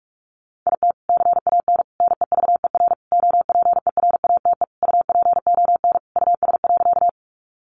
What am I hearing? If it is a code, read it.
Morse code, heard as a keyed tone, about 44 wpm, 700 Hz. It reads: IT CAN NEVER OPERATE UPON US.
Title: Sentences from Top 1500 Words 45wpm